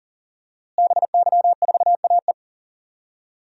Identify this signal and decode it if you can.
Morse code, heard as a keyed tone, about 40 wpm, 700 Hz. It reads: BY4AE
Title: DXCC-100 Call Signs - 5 Characters - Repeated 2x - 40wpm